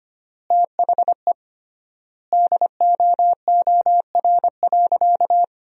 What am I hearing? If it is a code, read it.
Morse code, heard as a keyed tone, about 25 wpm, 700 Hz. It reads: THE DOOR.
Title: 3x Word-Spacing - Sentences from Top 300 Words - 25wpm